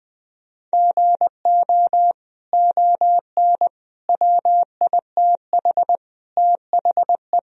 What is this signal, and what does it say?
Morse code, heard as a keyed tone, about 20 wpm, 700 Hz. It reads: GO ON WITH THE
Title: Sentences from Top 1000 Words - Review Part 2 - 20wpm